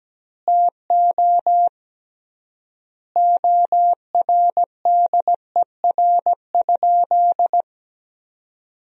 Morse code, heard as a keyed tone, about 17 wpm, 700 Hz. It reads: TO ORDER?
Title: 3x Word-Spacing - Sentences from Top 400 Words - 17wpm